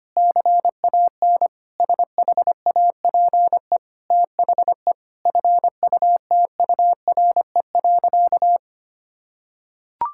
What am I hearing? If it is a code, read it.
Morse code, heard as a keyed tone, about 25 wpm, 700 Hz. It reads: CAN SHAPE THE FUTURE.
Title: Sentences from Top 1000 Words - Review Part 6 - 25wpm